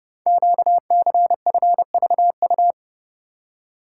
Morse code, heard as a keyed tone, about 30 wpm, 700 Hz. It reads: QCFVU
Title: Code Groups - 5 Letters - 30wpm